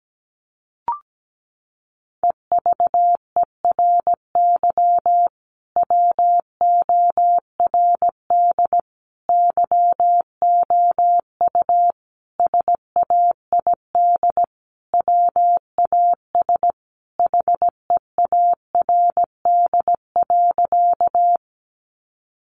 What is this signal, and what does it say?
Morse code, heard as a keyed tone, about 17 wpm, 700 Hz. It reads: EVERY WORD YOU SAID WAS HEARD.
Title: Sentences from Top 400 Words 17wpm